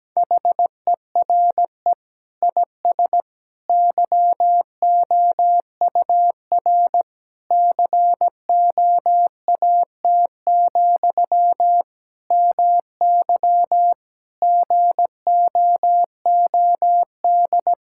Morse code, heard as a keyed tone, about 17 wpm, 700 Hz. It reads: HERE IS YOUR COAT, MY GOOD
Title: Sentences from Top 1600 Words 17wpm